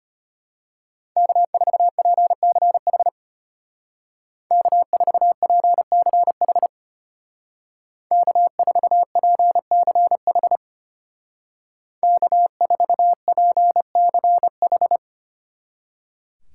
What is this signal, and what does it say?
Morse code, heard as a keyed tone, about 32 wpm, 700 Hz. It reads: K4PCH K4PCH K4PCH K4PCH